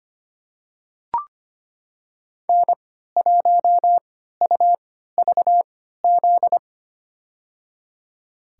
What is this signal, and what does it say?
Morse code, heard as a keyed tone, about 25 wpm, spaced as for 14 wpm, 700 Hz. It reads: N1UVZ
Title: USA Call Signs - 5 Characters - 3x Character Spacing - 25wpm